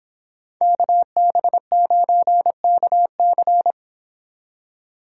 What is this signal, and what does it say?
Morse code, heard as a keyed tone, about 26 wpm, 700 Hz. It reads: KB9KC